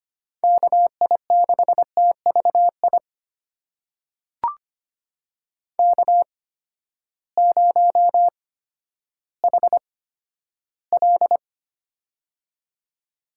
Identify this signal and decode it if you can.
Morse code, heard as a keyed tone, about 25 wpm, spaced as for 12 wpm, 700 Hz. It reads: KI6TVI K0HL
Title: USA Call Signs - 8x Character Spacing - 25wpm